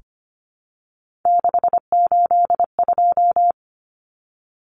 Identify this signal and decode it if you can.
Morse code, heard as a keyed tone, about 25 wpm, 700 Hz. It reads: 682
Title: Common Cut Numbers - 3-Digit 25wpm